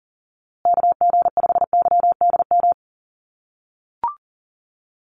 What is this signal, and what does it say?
Morse code, heard as a keyed tone, about 40 wpm, 700 Hz. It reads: KG5YDM
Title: USA Call Signs Repeated 3x 40wpm